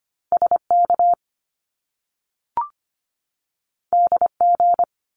Morse code, heard as a keyed tone, about 25 wpm, 700 Hz. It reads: SK DG
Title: Top Two Letter Combinations in Words 25wpm